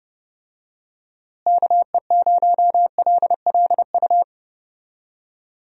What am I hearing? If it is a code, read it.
Morse code, heard as a keyed tone, about 30 wpm, 700 Hz. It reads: KE0LLU
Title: USA Call Signs - 6 Characters - Repeated 3x - 30wpm